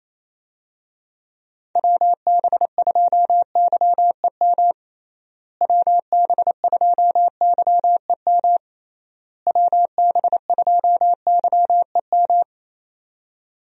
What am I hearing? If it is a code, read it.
Morse code, heard as a keyed tone, about 27 wpm, 700 Hz. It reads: WB2YEM WB2YEM WB2YEM
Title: USA Call Signs - 6 Characters - Repeated 3x - 28wpm